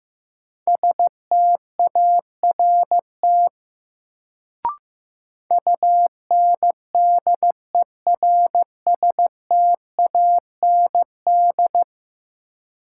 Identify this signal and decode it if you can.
Morse code, heard as a keyed tone, about 15 wpm, 700 Hz. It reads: START UNDERSTAND
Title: Sets of 4 Words - New Word Drill - 15wpm